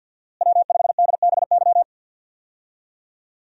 Morse code, heard as a keyed tone, about 52 wpm, 700 Hz. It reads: W5DBY
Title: USA Call Signs - 5 Characters - Repeated 2x - 50wpm